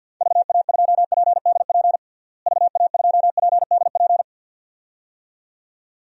Morse code, heard as a keyed tone, about 47 wpm, 700 Hz. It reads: VA2PDP VA2PDP
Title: DXCC-100 Call Signs - 6 Characters - Repeated 2x - 50wpm